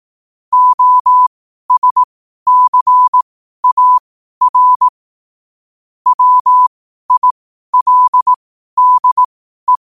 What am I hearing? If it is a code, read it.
Morse code, heard as a keyed tone, about 18 wpm, spaced as for 12 wpm, 1000 Hz. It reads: OSCAR WILDE
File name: qotdfiles / 12wpm